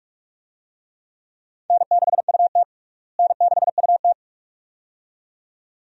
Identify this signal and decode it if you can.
Morse code, heard as a keyed tone, about 47 wpm, 700 Hz. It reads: N6UT N6UT
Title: USA Call Signs - 4 Characters - Repeated 2x - 45wpm